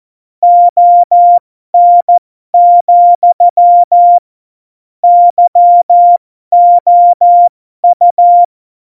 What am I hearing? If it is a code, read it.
Morse code, heard as a keyed tone, about 14 wpm, spaced as for 12 wpm, 700 Hz. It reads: ON, YOU
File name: Morsekurs / Witze